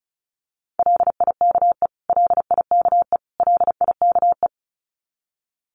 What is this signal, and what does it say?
Morse code, heard as a keyed tone, about 34 wpm, 700 Hz. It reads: LIKE LIKE LIKE
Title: Top 500 Words Repeated 35wpm